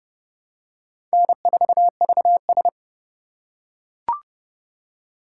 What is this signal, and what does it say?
Morse code, heard as a keyed tone, about 30 wpm, 700 Hz. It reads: N4VS
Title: USA Call Signs - 4 Characters - 4x Character Spacing - 30wpm